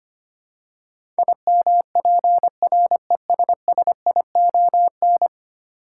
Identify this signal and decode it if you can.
Morse code, heard as a keyed tone, about 25 wpm, 700 Hz. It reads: IMPRESSION
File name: practice / 25wpm / Ten Letter Words - 25wpm